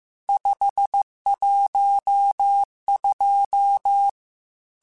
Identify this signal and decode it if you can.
Morse code, heard as a keyed tone, about 15 wpm, 800 Hz. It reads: 512
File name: Technical Subjects / Technical Subjects-mp3s / AND Gate @13